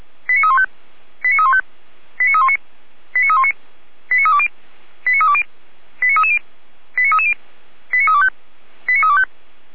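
Five-tone selective call (ZVEI), heard as quick runs of five tones, tones of about 70 ms. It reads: 89315 89315 89319 89319 89320 89320 89330 89330 89325 89325
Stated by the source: Five-tone ZVEI selective call received by SDR, from the audio server of Feuerwehr Orlamünde fire brigade.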